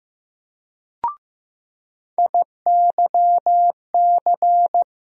Morse code, heard as a keyed tone, about 15 wpm, 700 Hz. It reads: IYC
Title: USA Call Sign Postfixes - 3 Letters 15wpm